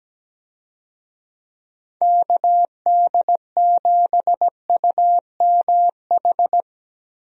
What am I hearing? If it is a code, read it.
Morse code, heard as a keyed tone, about 17 wpm, 700 Hz. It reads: KD7UMH
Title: USA Call Signs - 6 Characters - 17wpm